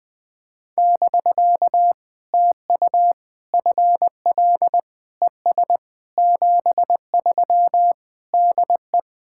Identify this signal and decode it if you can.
Morse code, heard as a keyed tone, about 20 wpm, 700 Hz. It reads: <BK> TU FL ES 73 DE
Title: POTA QSO Practice - 20wpm